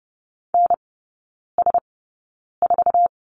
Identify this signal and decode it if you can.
Morse code, heard as a keyed tone, about 30 wpm, spaced as for 9 wpm, 700 Hz. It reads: NS4
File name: Lesson 17e - Letter u - Three Characters 30wpm